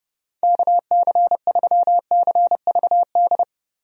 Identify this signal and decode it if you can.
Morse code, heard as a keyed tone, about 30 wpm, 700 Hz. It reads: KC3CVD
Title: USA Call Signs - 6 Characters - Repeated 3x - 30wpm